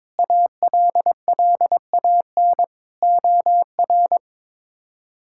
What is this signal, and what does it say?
Morse code, heard as a keyed tone, about 22 wpm, 700 Hz. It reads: ALLAN OR
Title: K1USN SST QSO Practice - 22wpm